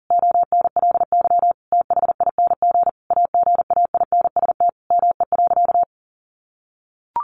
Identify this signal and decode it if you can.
Morse code, heard as a keyed tone, about 40 wpm, 700 Hz. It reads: ONLY THING AGAINST ME.
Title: Sentences from Top 1100 Words 40wpm